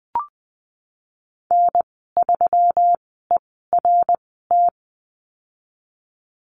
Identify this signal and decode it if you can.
Morse code, heard as a keyed tone, about 20 wpm, spaced as for 15 wpm, 700 Hz. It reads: N3ERT
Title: 128 Lesson 21g - Number 6 - Call Signs 20wpm